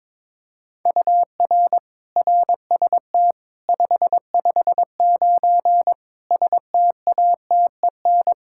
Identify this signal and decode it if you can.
Morse code, heard as a keyed tone, about 22 wpm, 700 Hz. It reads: UR RST 559 STATEN
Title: Practice QSO - Exchange 1 - Beginning - 22wpm